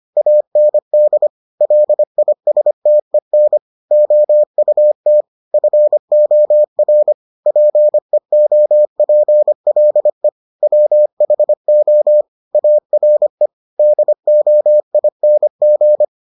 Morse code, heard as a keyed tone, about 25 wpm, 600 Hz. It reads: AND LISTEN OUT FOR PEOPLE WHO ARE DOING